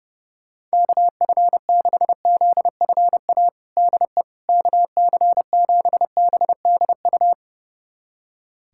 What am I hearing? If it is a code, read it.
Morse code, heard as a keyed tone, about 30 wpm, 700 Hz. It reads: KF6ZFA DE KC7BDU
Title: USA Callsign DE Callsign - 6 Characters - 30wpm